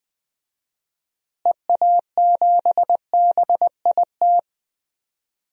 Morse code, heard as a keyed tone, about 20 wpm, 700 Hz. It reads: EA7BIT
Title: DXCC-100 Call Signs - 6 Characters - Repeated 2x - 20wpm